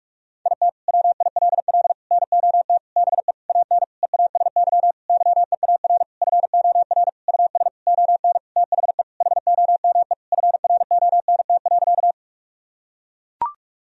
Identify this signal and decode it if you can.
Morse code, heard as a keyed tone, about 41 wpm, 700 Hz. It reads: IT WILL NOT BE AN EASY YEAR FOR US ON THE HOME FRONT.
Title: Sentences from Top 1000 Words - Review Part 2 - 45wpm